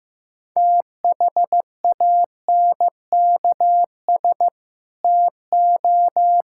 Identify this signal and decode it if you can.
Morse code, heard as a keyed tone, about 15 wpm, 700 Hz. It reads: THANKS TO
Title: Sentences from Top 1000 Words - Review Part 5 - 15wpm